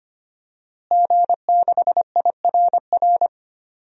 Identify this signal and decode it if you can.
Morse code, heard as a keyed tone, about 25 wpm, 700 Hz. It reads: G6IRR
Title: DXCC-100 Call Signs - 5 Characters - Repeated 3x - 25wpm